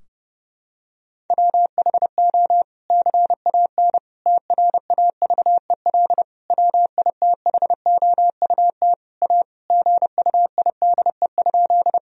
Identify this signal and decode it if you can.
Morse code, heard as a keyed tone, about 30 wpm, 700 Hz. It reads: WHO CAN TRAVEL WITHOUT A GUIDE?
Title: Sentences from Top 1000 Words - Review Part 3 - 30wpm